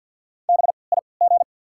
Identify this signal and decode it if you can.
Morse code, heard as a keyed tone, about 54 wpm, spaced as for 29 wpm, 690 Hz. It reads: BIG
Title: 3x Character Spacing - Top 300 Words - 50wpm